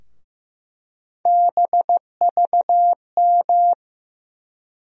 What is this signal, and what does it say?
Morse code, heard as a keyed tone, about 15 wpm, 700 Hz. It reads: BVM
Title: USA Call Sign Postfixes - 3 Letters 15wpm